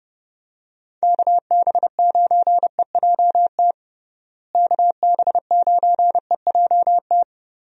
Morse code, heard as a keyed tone, about 32 wpm, 700 Hz. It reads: KB9EJT KB9EJT
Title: USA Call Signs - 6 Characters - Repeated 3x - 30wpm